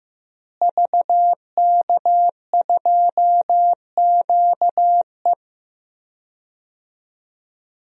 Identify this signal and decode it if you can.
Morse code, heard as a keyed tone, about 15 wpm, 700 Hz. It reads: VK2QE
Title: DXCC-100 Call Signs - 5 Characters - Repeated 3x - 15wpm